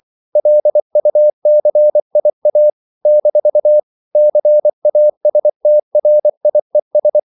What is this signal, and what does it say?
Morse code, heard as a keyed tone, about 24 wpm, 600 Hz. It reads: LUCIA - CASTRIES
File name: CW / Practice / NationalCapitals24